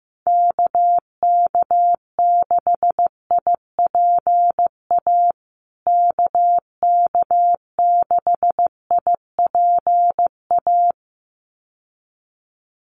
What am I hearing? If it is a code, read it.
Morse code, heard as a keyed tone, about 15 wpm, 700 Hz. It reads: KK6IPA KK6IPA
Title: USA Call Signs Repeated 2x 15wpm